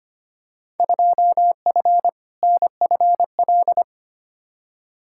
Morse code, heard as a keyed tone, about 25 wpm, 700 Hz. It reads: 2F NFL
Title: Field Day Exchanges - 25wpm